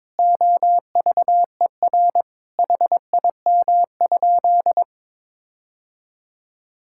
Morse code, heard as a keyed tone, about 22 wpm, 700 Hz. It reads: OVER HIM?
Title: Sentences from Top 1600 Words 22wpm